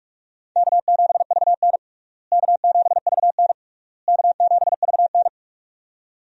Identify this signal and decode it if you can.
Morse code, heard as a keyed tone, about 46 wpm, 700 Hz. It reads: K7VN K7VN K7VN
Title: USA Call Signs - 4 Characters - Repeated 3x - 45wpm